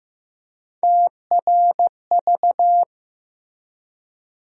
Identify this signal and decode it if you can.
Morse code, heard as a keyed tone, about 15 wpm, 700 Hz. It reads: TRV
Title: USA Call Sign Postfixes - 3 Letters 15wpm